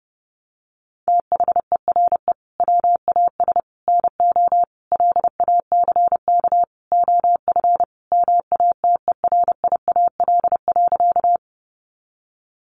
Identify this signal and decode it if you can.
Morse code, heard as a keyed tone, about 30 wpm, 700 Hz. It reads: THERE WAS NO LACK OF MATERIAL.
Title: Sentences from Top 1100 Words 30wpm